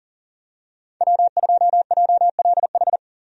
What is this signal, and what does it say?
Morse code, heard as a keyed tone, about 40 wpm, 700 Hz. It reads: W2JLH
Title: USA Call Signs - 5 Characters - 40wpm